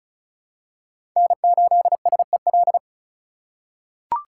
Morse code, heard as a keyed tone, about 34 wpm, 700 Hz. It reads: N8SEL
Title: USA Call Signs - 5 Characters - 3x Character Spacing - 35wpm